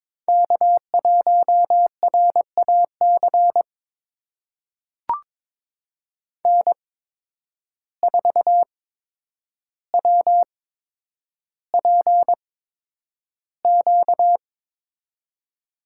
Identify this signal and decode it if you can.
Morse code, heard as a keyed tone, about 22 wpm, spaced as for 9 wpm, 700 Hz. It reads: K1RAC N4WPQ
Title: USA Call Signs - 5 Characters - 8x Character Spacing - 22wpm